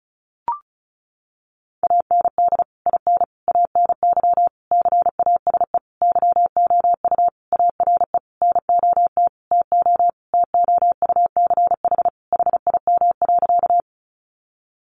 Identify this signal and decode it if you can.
Morse code, heard as a keyed tone, about 34 wpm, 700 Hz. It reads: AND IN ANY CASE YOU ARE NOT TO TOUCH HIM.
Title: Sentences from Top 1200 Words 35wpm